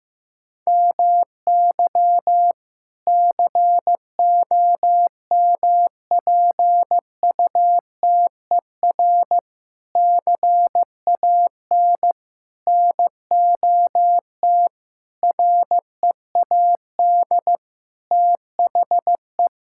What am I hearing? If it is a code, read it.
Morse code, heard as a keyed tone, about 15 wpm, 700 Hz. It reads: MY COMPUTER CAN NOT READ THE
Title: Sentences from Top 1000 Words - Review Part 6 - 15wpm